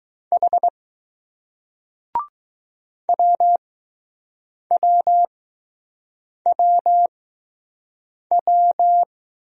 Morse code, heard as a keyed tone, about 20 wpm, spaced as for 6 wpm, 700 Hz. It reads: H WWWW